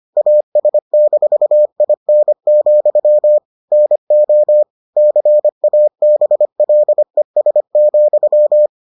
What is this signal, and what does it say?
Morse code, heard as a keyed tone, about 25 wpm, 600 Hz. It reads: AS-IN, NO CABLES,